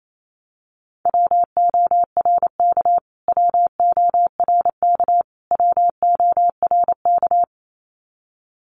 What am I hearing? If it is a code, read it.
Morse code, heard as a keyed tone, about 28 wpm, 700 Hz. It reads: WORK WORK WORK
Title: Top 300 Words Repeated 28wpm